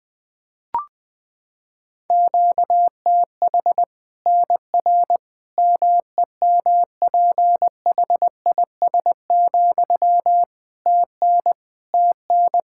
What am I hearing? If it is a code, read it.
Morse code, heard as a keyed tone, about 20 wpm, 700 Hz. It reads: QTH NR MEMPHIS, TN TN
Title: Practice QSO - Exchange 1 - Beginning - 20wpm